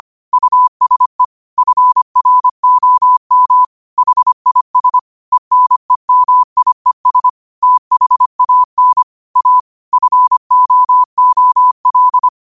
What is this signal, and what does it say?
Morse code, heard as a keyed tone, about 25 wpm, 1000 Hz. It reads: USE FROM HIS ENEMIES THAN A FOOL